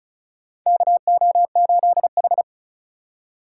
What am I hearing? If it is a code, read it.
Morse code, heard as a keyed tone, about 36 wpm, 700 Hz. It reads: KO8H
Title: USA Call Signs - 4x Character Spacing - 35wpm